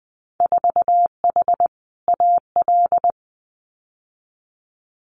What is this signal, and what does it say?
Morse code, heard as a keyed tone, about 20 wpm, 700 Hz. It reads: 4H AL
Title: Winter Field Day Exchanges Encore 20wpm